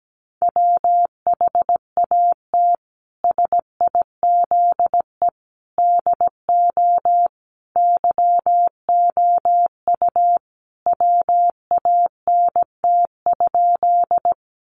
Morse code, heard as a keyed tone, about 17 wpm, 700 Hz. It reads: WHAT SIZE DO YOU WANT?
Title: Sentences from Top 400 Words 17wpm